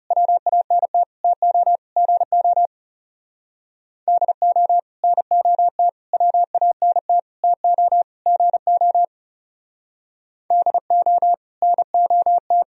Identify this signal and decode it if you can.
Morse code, heard as a keyed tone, about 37 wpm, 700 Hz. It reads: WANT TO GO DO NOT WANT TO GO DO NOT
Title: Speed-Racing - Sets of 5 Words 30wpm